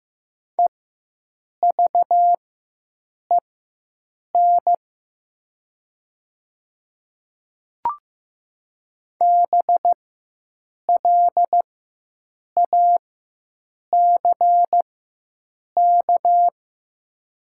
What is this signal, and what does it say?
Morse code, heard as a keyed tone, about 15 wpm, spaced as for 7 wpm, 690 Hz. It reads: EVEN BLACK
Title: 4x Character Spacing - Top 500 Words - 15wpm